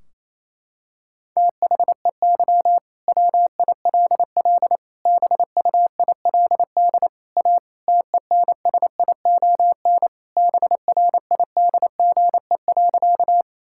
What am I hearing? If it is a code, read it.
Morse code, heard as a keyed tone, about 28 wpm, 700 Hz. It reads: THEY WILL BUILD A TENSION BRIDGE.
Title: Sentences from Top 1600 Words 28wpm